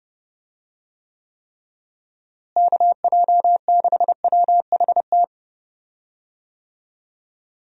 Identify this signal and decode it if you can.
Morse code, heard as a keyed tone, about 30 wpm, 700 Hz. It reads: KJ6WHT
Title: USA Call Signs - 6 Characters - Repeated 3x - 30wpm